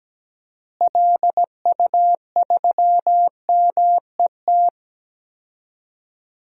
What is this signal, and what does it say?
Morse code, heard as a keyed tone, about 17 wpm, 700 Hz. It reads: LU3MET
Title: DXCC-100 Call Signs - 6 Characters - Repeated 3x - 17wpm